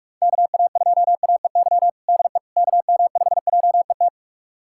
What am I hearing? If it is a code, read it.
Morse code, heard as a keyed tone, about 45 wpm, 700 Hz. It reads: KA2AEY DE KM5JET
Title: USA Callsign DE Callsign - 6 Characters - 45wpm